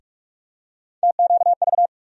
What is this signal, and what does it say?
Morse code, heard as a keyed tone, about 45 wpm, 700 Hz. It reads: TQV
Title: USA Call Sign Postfixes - 3 Letters 45wpm